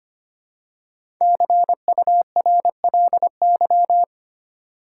Morse code, heard as a keyed tone, about 25 wpm, 700 Hz. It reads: CURLY
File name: 5 Letter Words - First 3 Letters Given - 25wpm